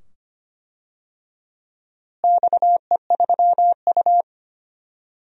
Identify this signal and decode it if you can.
Morse code, heard as a keyed tone, about 25 wpm, 700 Hz. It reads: XE3U
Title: DXCC-100 Call Signs - 4 Characters - Repeated 2x - 25wpm